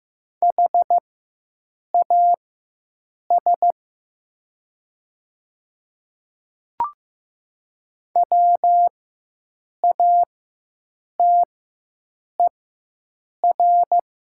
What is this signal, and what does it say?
Morse code, heard as a keyed tone, about 15 wpm, spaced as for 7 wpm, 700 Hz. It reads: HAS WATER
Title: 4x Character Spacing - Top 100 Words - 15wpm